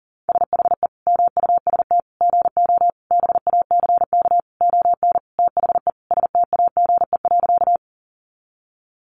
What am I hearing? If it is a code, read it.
Morse code, heard as a keyed tone, about 40 wpm, 700 Hz. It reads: SHE MUST GO BACK ON THE STAGE.
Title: Sentences from Top 1200 Words 40wpm